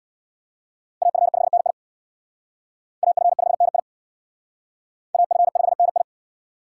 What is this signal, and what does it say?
Morse code, heard as a keyed tone, about 61 wpm, 700 Hz. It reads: AF5NI AF5NI AF5NI
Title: Speed Racing - USA 2x2 Call Signs 50wpm